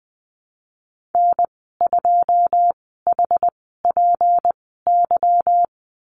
Morse code, heard as a keyed tone, about 20 wpm, spaced as for 15 wpm, 700 Hz. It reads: N2HPY
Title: 173 Lesson 28g - Number 9 - Call Signs 20wpm